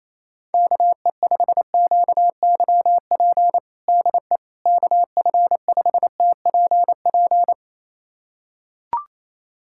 Morse code, heard as a keyed tone, about 28 wpm, 700 Hz. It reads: KE5QYP DE KF5TPP
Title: USA Callsign DE Callsign - 6 Characters - 28wpm